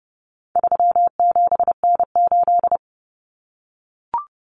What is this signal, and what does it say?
Morse code, heard as a keyed tone, about 30 wpm, 700 Hz. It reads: 37N8
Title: Common Cut Numbers - 4-Digit 30wpm